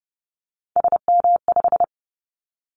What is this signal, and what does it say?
Morse code, heard as a keyed tone, about 30 wpm, 700 Hz. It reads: SM5
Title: Callsign Prefixes for Top 25 DXCC - 30wpm